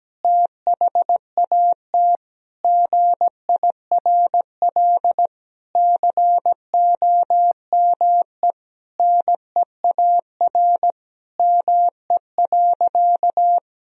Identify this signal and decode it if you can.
Morse code, heard as a keyed tone, about 17 wpm, 700 Hz. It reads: THAT GIRL COME NEAR ME.
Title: Sentences from Top 1000 Words - Review Part 1 - 17wpm